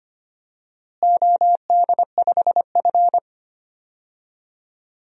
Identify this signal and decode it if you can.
Morse code, heard as a keyed tone, about 25 wpm, 700 Hz. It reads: OD5F
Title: DXCC-100 Call Signs - 4 Characters - Repeated 3x - 25wpm